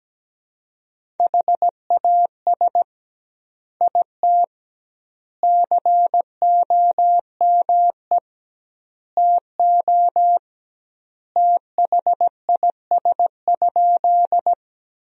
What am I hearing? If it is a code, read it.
Morse code, heard as a keyed tone, about 17 wpm, 700 Hz. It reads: HAS IT COME TO THIS?
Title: 2x Word-Spacing - Sentences from Top 100 Words - 17wpm